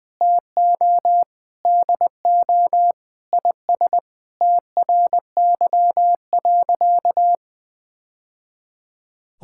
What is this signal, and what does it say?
Morse code, heard as a keyed tone, about 20 wpm, 700 Hz. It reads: TO DO IS TRY.
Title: Sentences with 3-Word Phrases - Spoken Beginning - 20wpm